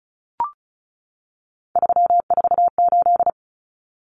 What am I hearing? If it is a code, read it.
Morse code, heard as a keyed tone, about 34 wpm, 700 Hz. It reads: 348
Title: Common Cut Numbers - 3-Digit 35wpm